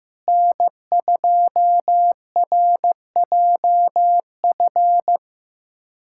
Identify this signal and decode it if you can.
Morse code, heard as a keyed tone, about 15 wpm, 700 Hz. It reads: N2RJF
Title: USA Callsign DE Callsign - 5 Characters - 15wpm